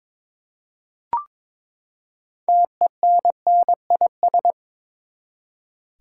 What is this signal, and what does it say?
Morse code, heard as a keyed tone, about 22 wpm, 700 Hz. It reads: TENNIS
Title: Six Letter Words - 22wpm